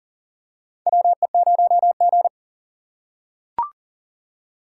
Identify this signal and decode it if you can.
Morse code, heard as a keyed tone, about 40 wpm, 690 Hz. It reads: WE0G
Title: USA Call Signs - 4 Characters - 2x Character Spacing - 40wpm